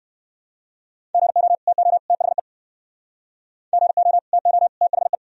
Morse code, heard as a keyed tone, about 58 wpm, 700 Hz. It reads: GO TO THE GO TO THE